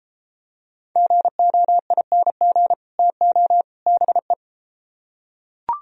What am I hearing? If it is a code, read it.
Morse code, heard as a keyed tone, about 33 wpm, 700 Hz. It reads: GOING TO BE